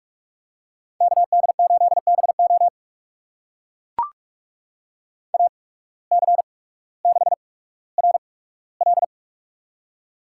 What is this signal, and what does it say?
Morse code, heard as a keyed tone, about 44 wpm, spaced as for 19 wpm, 700 Hz. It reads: KD8BO AC6RL
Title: USA Call Signs - 5 Characters - 8x Character Spacing - 45wpm